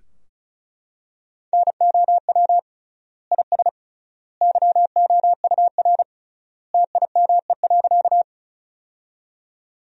Morse code, heard as a keyed tone, about 34 wpm, 700 Hz. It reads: NOW IS YOUR TIME.
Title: 3x Word-Spacing - Sentences from Top 100 Words - 35wpm